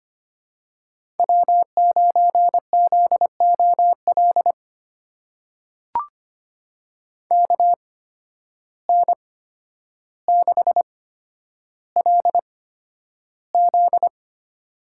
Morse code, heard as a keyed tone, about 25 wpm, spaced as for 11 wpm, 700 Hz. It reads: W9ZOL KN6LZ
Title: USA Call Signs - 5 Characters - 8x Character Spacing - 25wpm